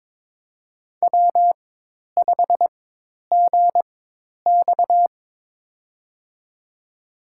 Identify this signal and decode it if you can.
Morse code, heard as a keyed tone, about 22 wpm, spaced as for 10 wpm, 700 Hz. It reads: W5GX
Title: USA Call Signs - 4 Characters - 4x Character Spacing - 22wpm